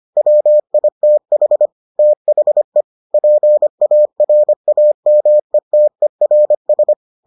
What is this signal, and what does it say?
Morse code, heard as a keyed tone, about 25 wpm, 600 Hz. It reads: WITH THE PARAMETERS